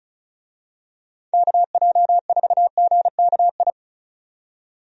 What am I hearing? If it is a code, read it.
Morse code, heard as a keyed tone, about 36 wpm, 700 Hz. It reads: KJ4GKI
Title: USA Call Signs - 6 Characters - 35wpm